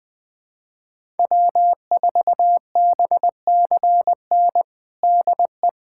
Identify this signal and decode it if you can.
Morse code, heard as a keyed tone, about 20 wpm, 700 Hz. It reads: W4BCN DE
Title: USA Callsign DE Callsign - 5 Characters - 20wpm